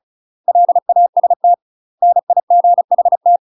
Morse code, heard as a keyed tone, about 34 wpm, 700 Hz. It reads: LAST NIGHT
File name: UK MASTER QSOs TRAINING / 35WPM / UK MASTER QSOs 0043